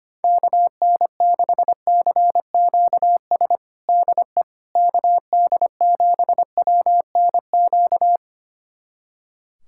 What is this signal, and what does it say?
Morse code, heard as a keyed tone, about 25 wpm, 700 Hz. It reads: KN6CQS DE KD7WNQ
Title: USA Callsign DE Callsign - 6 Characters - 25wpm